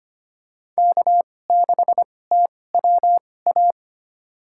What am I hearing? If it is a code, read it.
Morse code, heard as a keyed tone, about 25 wpm, spaced as for 18 wpm, 700 Hz. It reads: K6TWA
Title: USA Call Signs - 2x Character Spacing - 25wpm